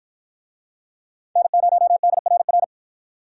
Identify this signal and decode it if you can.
Morse code, heard as a keyed tone, about 47 wpm, 700 Hz. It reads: N0DRR